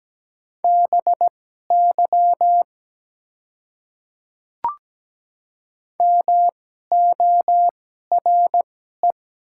Morse code, heard as a keyed tone, about 17 wpm, spaced as for 12 wpm, 700 Hz. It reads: BY MORE
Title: 2x Character Spacing - Top 100 Words - 17wpm